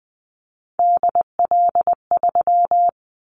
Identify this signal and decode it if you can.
Morse code, heard as a keyed tone, about 20 wpm, 700 Hz. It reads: DL3
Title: Callsign Prefixes for Top 50 DXCC - 20wpm